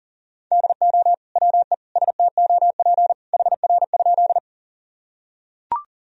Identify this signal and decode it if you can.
Morse code, heard as a keyed tone, about 37 wpm, 700 Hz. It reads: DO WE STOP HR?
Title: Sentences from Top 300 Words and Abbreviations 40wpm